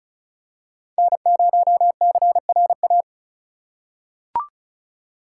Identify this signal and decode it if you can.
Morse code, heard as a keyed tone, about 34 wpm, 700 Hz. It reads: N0CRA
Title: USA Call Signs - 2x Character Spacing - 35wpm